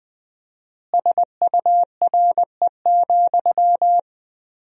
Morse code, heard as a keyed tone, about 20 wpm, 700 Hz. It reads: SURE,
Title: 4x Word-Spacing - Sentences from Top 500 Words - 20wpm